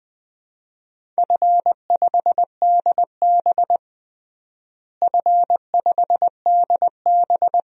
Morse code, heard as a keyed tone, about 20 wpm, 700 Hz. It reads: F5DB F5DB
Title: DXCC-100 Call Signs - 4 Characters - Repeated 2x - 20wpm